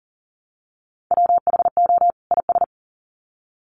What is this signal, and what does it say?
Morse code, heard as a keyed tone, about 40 wpm, 700 Hz. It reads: WHO IS
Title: Sets of 2 Words 40wpm